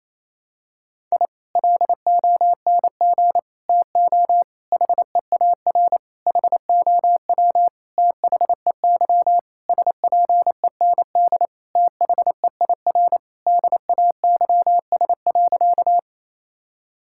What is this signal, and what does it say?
Morse code, heard as a keyed tone, about 28 wpm, 700 Hz. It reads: I LONG TO HEAR HOW THEY SPEND THEIR DAYS.
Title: Sentences from Top 1000 Words - Review Part 5 - 28wpm